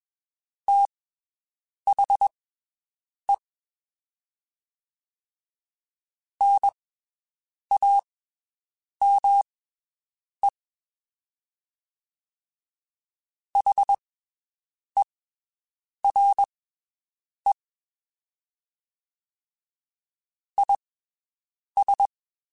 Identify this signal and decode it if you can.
Morse code, heard as a keyed tone, about 22 wpm, spaced as for 7 wpm, 800 Hz. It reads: THE NAME HERE IS
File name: (spaced) / QSOs Spaced-mp3s / WD0BGP 1 @20 sp